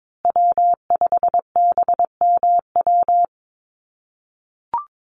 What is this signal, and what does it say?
Morse code, heard as a keyed tone, about 22 wpm, 700 Hz. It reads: W5BMW
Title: USA Call Signs Repeated 2x 22wpm